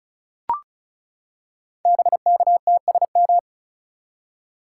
Morse code, heard as a keyed tone, about 36 wpm, 690 Hz. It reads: BKTSM
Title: Code Groups - 5 Letters - 35wpm